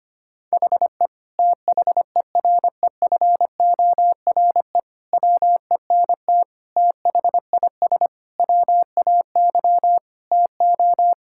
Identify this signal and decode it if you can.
Morse code, heard as a keyed tone, about 25 wpm, 700 Hz. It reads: HE THEREFORE WENT THIS WAY TO
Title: Sentences from Top 1000 Words - Review Part 5 - 25wpm